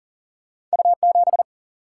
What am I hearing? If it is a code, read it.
Morse code, heard as a keyed tone, about 40 wpm, 700 Hz. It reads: U 7
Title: Single-Letter Single-Number 40wpm